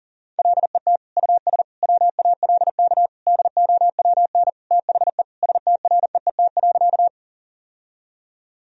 Morse code, heard as a keyed tone, about 40 wpm, 700 Hz. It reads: LET US WALK DOWN THE STREET.
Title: Sentences from Top 1000 Words - Review Part 3 - 40wpm